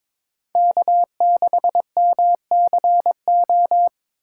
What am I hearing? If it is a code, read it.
Morse code, heard as a keyed tone, about 22 wpm, 700 Hz. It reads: K6MCO
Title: USA Call Signs - 5 Characters - 3x Character Spacing - 22wpm